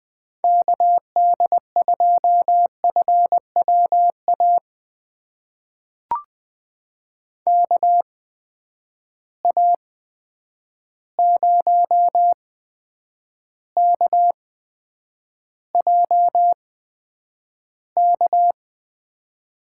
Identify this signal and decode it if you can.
Morse code, heard as a keyed tone, about 20 wpm, spaced as for 9 wpm, 700 Hz. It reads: KD2FWA KA0KJK
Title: USA Call Signs - 6 Characters - 8x Character Spacing - 20wpm